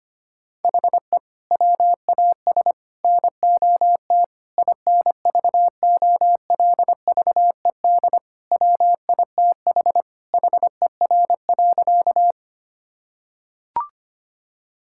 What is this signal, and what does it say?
Morse code, heard as a keyed tone, about 25 wpm, 700 Hz. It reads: HE WAS NOT INVOLVED WITH HER.
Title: Sentences from Top 1000 Words - Review Part 5 - 25wpm